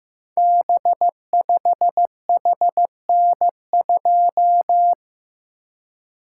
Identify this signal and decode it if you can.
Morse code, heard as a keyed tone, about 15 wpm, 700 Hz. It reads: B5HN2
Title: Code Groups - 5 Letter:Numbers - 15wpm